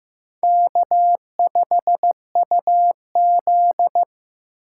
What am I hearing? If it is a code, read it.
Morse code, heard as a keyed tone, about 15 wpm, 700 Hz. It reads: K5UZ
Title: USA Call Signs - 4 Characters - 4x Character Spacing - 15wpm